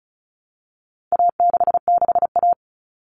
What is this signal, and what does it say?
Morse code, heard as a keyed tone, about 34 wpm, 700 Hz. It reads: A66A
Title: All Cut Numbers - 4-Digit 35wpm